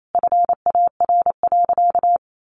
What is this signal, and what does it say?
Morse code, heard as a keyed tone, about 28 wpm, 700 Hz. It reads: FAR.
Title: Sentences from Top 800 Words 28wpm